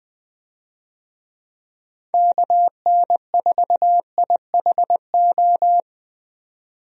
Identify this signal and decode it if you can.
Morse code, heard as a keyed tone, about 20 wpm, 700 Hz. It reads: KN4IHO
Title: USA Call Signs - 6 Characters - 20wpm